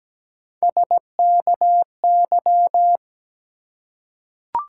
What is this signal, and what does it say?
Morse code, heard as a keyed tone, about 17 wpm, 700 Hz. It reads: SKY
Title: Three Letter Words - 17wpm